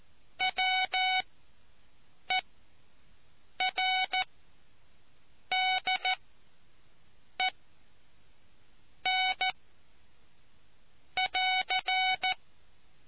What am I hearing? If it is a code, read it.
Morse code, heard as a keyed tone, about 14 wpm, spaced as for 6 wpm, 750 Hz. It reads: WERDEN <AR>